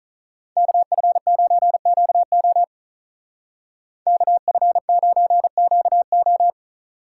Morse code, heard as a keyed tone, about 43 wpm, 700 Hz. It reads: KF9QO KF9QO
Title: Speed Racing - USA 2x2 Call Signs 35wpm